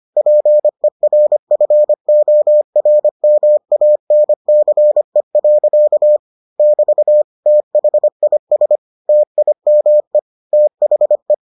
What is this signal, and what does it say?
Morse code, heard as a keyed tone, about 25 wpm, 600 Hz. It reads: PERFORMANCE. <BT> THIS TIME THE